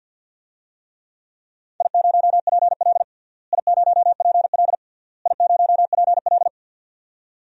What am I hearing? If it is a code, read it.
Morse code, heard as a keyed tone, about 48 wpm, 700 Hz. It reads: I0PL I0PL I0PL
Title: DXCC-100 Call Signs - 4 Characters - Repeated 3x - 50wpm